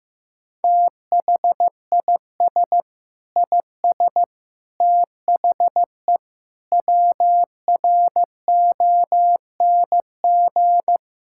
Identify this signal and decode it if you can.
Morse code, heard as a keyed tone, about 15 wpm, 700 Hz. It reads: THIS IS THE WRONG
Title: Sentences from Top 1000 Words - Review Part 6 - 15wpm